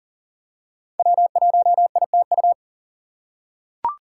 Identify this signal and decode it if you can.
Morse code, heard as a keyed tone, about 40 wpm, 700 Hz. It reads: W1ITU
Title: USA Call Signs - 5 Characters - 2x Character Spacing - 40wpm